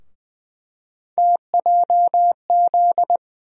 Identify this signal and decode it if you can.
Morse code, heard as a keyed tone, about 21 wpm, 700 Hz. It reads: TJZ
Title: USA Call Sign Postfixes - 3 Letters 20wpm